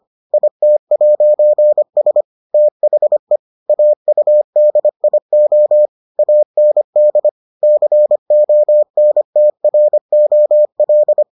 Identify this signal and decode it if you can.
Morse code, heard as a keyed tone, about 25 wpm, 600 Hz. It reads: IT'S THE AUDIO AND CONTROL